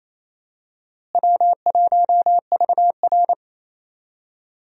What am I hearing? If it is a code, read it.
Morse code, heard as a keyed tone, about 28 wpm, 700 Hz. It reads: W1VR
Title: USA Call Signs - 4 Characters - Repeated 2x - 28wpm